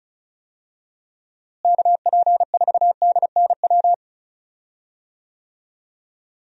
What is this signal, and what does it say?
Morse code, heard as a keyed tone, about 36 wpm, 700 Hz. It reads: KP4DNW
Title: USA Call Signs - 6 Characters - Repeated 3x - 35wpm